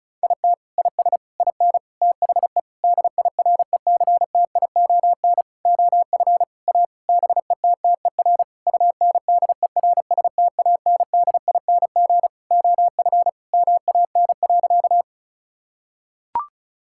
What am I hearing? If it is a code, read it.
Morse code, heard as a keyed tone, about 36 wpm, 700 Hz. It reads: IT IS IN THE DIRECTION OF A BETTER UNDERSTANDING OF MAN.
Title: Sentences from Top 1000 Words - Review Part 5 - 35wpm